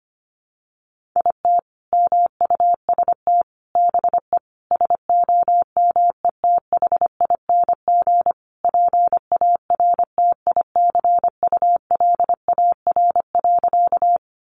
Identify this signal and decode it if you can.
Morse code, heard as a keyed tone, about 25 wpm, 700 Hz. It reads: IT MUST BE SOMETHING PARTICULAR.
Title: Sentences from Top 600 Words 25wpm